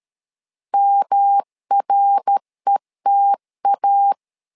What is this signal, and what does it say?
Morse code, heard as a keyed tone, about 13 wpm, 800 Hz. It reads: MRETA
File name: LEZIONI / 3 MWRO / 3 MWR 12wpm 1711